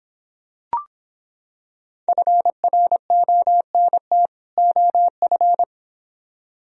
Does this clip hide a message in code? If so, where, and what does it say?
Yes: Morse code from 2.1 to 5.6 s, heard as a keyed tone, about 26 wpm, 700 Hz. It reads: FRONT OF